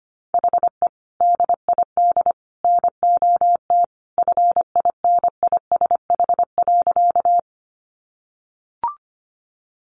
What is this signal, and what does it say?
Morse code, heard as a keyed tone, about 25 wpm, 700 Hz. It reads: HE DID NOT FINISH.
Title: Sentences from Top 1200 Words 25wpm